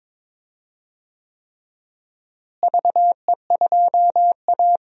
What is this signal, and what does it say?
Morse code, heard as a keyed tone, about 22 wpm, 700 Hz. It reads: VE2A
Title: DXCC-100 Call Signs - 4 Characters - Repeated 3x - 22wpm